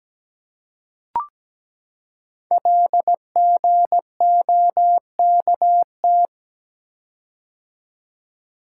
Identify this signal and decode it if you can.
Morse code, heard as a keyed tone, about 17 wpm, 700 Hz. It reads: LGOKT
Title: Code Groups - 5 Letters - 17wpm